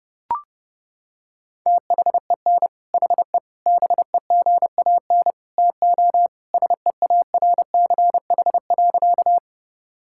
Morse code, heard as a keyed tone, about 30 wpm, 700 Hz. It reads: THEN HE BEGAN TO SEARCH.
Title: Sentences from Top 1000 Words - Review Part 5 - 30wpm